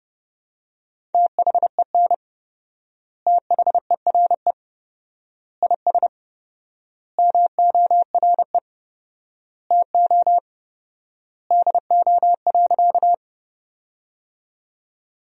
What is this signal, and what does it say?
Morse code, heard as a keyed tone, about 30 wpm, 700 Hz. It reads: THEN THERE IS MORE TO DO.
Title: 4x Word-Spacing - Sentences from Top 100 Words - 30wpm